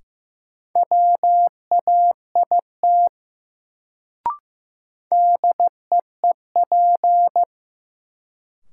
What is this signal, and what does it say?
Morse code, heard as a keyed tone, about 15 wpm, 700 Hz. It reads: WAIT DEEP
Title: Sentences from Top 500 Words - New Word Drill - 15wpm